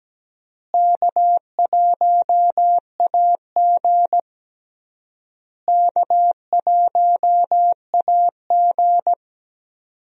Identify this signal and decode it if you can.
Morse code, heard as a keyed tone, about 17 wpm, 700 Hz. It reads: K1AG K1AG
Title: USA Call Signs - 4 Characters - Repeated 3x - 17wpm